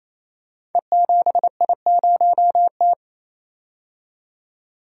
Morse code, heard as a keyed tone, about 27 wpm, 700 Hz. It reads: E7I0T
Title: Code Groups - 5 Letter:Numbers - 28wpm